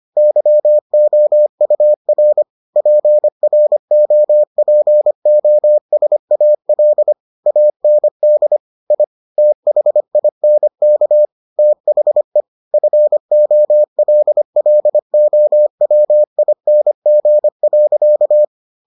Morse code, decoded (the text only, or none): YOUR PROPOSAL AND I THINK THE FOLLOWING.